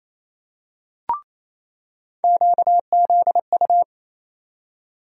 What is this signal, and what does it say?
Morse code, heard as a keyed tone, about 28 wpm, 700 Hz. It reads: QZU
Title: USA Call Sign Postfixes - 3 Letters 28wpm